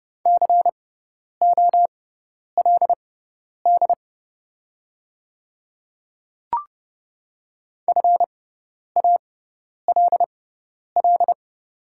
Morse code, heard as a keyed tone, about 30 wpm, spaced as for 10 wpm, 700 Hz. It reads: COLD FALL
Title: 6x Character Spacing - Top 500 Words - 30wpm